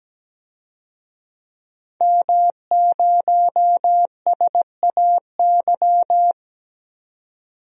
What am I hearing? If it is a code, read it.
Morse code, heard as a keyed tone, about 17 wpm, 700 Hz. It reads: M0SAY
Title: DXCC-100 Call Signs - 5 Characters - 17wpm